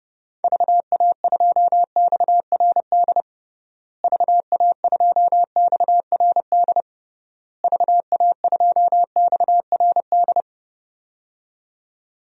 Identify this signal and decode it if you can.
Morse code, heard as a keyed tone, about 32 wpm, 700 Hz. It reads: VA2XRD VA2XRD VA2XRD
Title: DXCC-100 Call Signs - 6 Characters - Repeated 3x - 30wpm